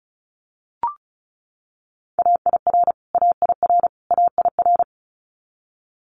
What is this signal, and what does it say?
Morse code, heard as a keyed tone, about 34 wpm, 700 Hz. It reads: AIR AIR AIR
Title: Top 500 Words Repeated 35wpm